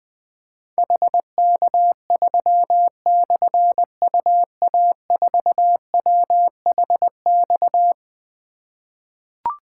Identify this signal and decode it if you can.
Morse code, heard as a keyed tone, about 20 wpm, 700 Hz. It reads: HK3/UA4WHX
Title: Difficult Call Signs 20wpm